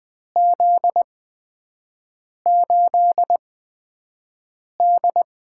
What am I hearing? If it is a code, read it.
Morse code, heard as a keyed tone, about 20 wpm, spaced as for 5 wpm, 700 Hz. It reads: Z8D
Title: USA Call Signs - 8x Character Spacing - 20wpm